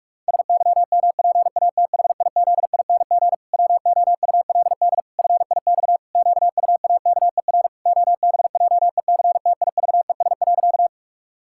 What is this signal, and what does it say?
Morse code, heard as a keyed tone, about 46 wpm, 700 Hz. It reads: SYMPATHIZING WOULD FIX QUAKER OBJECTIVES.
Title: Pangram Challenge v2 - 45wpm